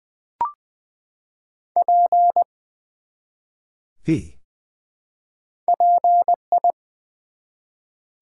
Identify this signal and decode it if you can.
Morse code, heard as a keyed tone, about 20 wpm, 700 Hz. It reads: P PI
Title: Build A Word - 1 Letter At A Time - 5-Letter Words - 20wpm